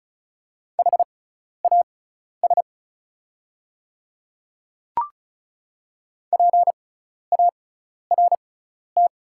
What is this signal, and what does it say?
Morse code, heard as a keyed tone, about 36 wpm, spaced as for 12 wpm, 690 Hz. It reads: HAS PART
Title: 6x Character Spacing - Top 200 Words - 35wpm